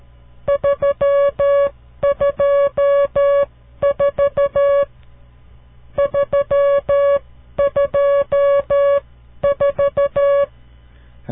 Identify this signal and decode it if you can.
Morse code, heard as a keyed tone, about 13 wpm, spaced as for 11 wpm, 560 Hz. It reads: 324 324
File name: CW Learning Programs / farnsworth / unit 5